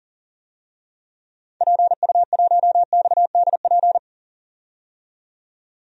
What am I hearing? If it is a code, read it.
Morse code, heard as a keyed tone, about 40 wpm, 700 Hz. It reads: PU1XDP
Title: DXCC-100 Call Signs - 6 Characters - 40wpm